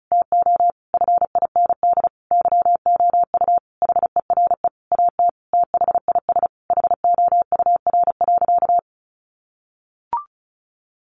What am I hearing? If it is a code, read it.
Morse code, heard as a keyed tone, about 34 wpm, 700 Hz. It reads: TO FIND YOU HERE AT THIS HOUR.
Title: Sentences from Top 1400 Words 35wpm